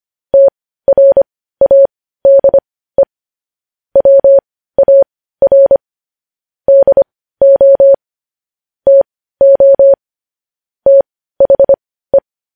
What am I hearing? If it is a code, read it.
Morse code, heard as a keyed tone, about 25 wpm, spaced as for 15 wpm, 550 Hz. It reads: TRADE WAR DO TO THE
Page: news-headlines-in-morse-code-at-15-wpm